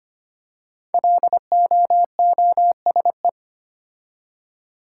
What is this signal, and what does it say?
Morse code, heard as a keyed tone, about 25 wpm, 700 Hz. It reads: LOOSE
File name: 5 Letter Words - First 3 Letters Given - 25wpm